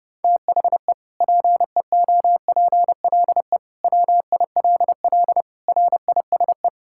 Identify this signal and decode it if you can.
Morse code, heard as a keyed tone, about 30 wpm, 700 Hz. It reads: THE PEOPLE WILL RISE
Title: Sentences from Top 1000 Words - Review Part 6 - 30wpm